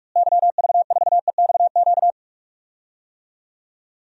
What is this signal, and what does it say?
Morse code, heard as a keyed tone, about 46 wpm, 700 Hz. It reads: YV4EXQ
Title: DXCC-100 Call Signs - 6 Characters - Repeated 2x - 45wpm